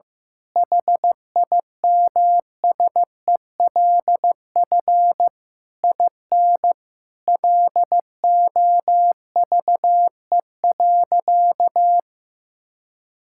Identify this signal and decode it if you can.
Morse code, heard as a keyed tone, about 15 wpm, 700 Hz. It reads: HIMSELF IN LOVE.
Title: Sentences from Top 1600 Words 15wpm